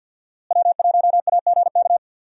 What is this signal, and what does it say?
Morse code, heard as a keyed tone, about 52 wpm, 700 Hz. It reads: W1AGK
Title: USA Call Signs - 5 Characters - Repeated 2x - 50wpm